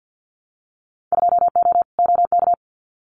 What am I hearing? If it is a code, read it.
Morse code, heard as a keyed tone, about 50 wpm, 700 Hz. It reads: 2O OK
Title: Winter Field Day Exchanges Encore 50wpm